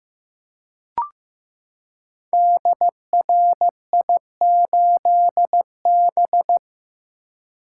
Morse code, heard as a keyed tone, about 15 wpm, 700 Hz. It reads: DRI8B
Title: Code Groups - 5 Letter:Numbers - 15wpm